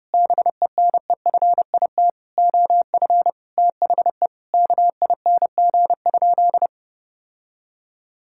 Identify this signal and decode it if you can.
Morse code, heard as a keyed tone, about 30 wpm, 700 Hz. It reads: BENEFIT OF THE KING?
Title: Sentences from Top 1000 Words - Review Part 6 - 30wpm